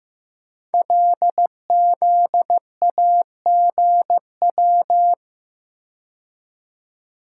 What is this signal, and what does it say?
Morse code, heard as a keyed tone, about 15 wpm, 700 Hz. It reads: LZAGW
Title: Code Groups - 5 Letter:Numbers - 15wpm